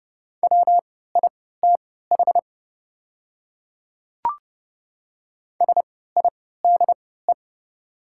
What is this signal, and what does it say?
Morse code, heard as a keyed tone, about 30 wpm, spaced as for 17 wpm, 700 Hz. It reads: WITH SIDE
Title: 3x Character Spacing - Top 200 Words - 30wpm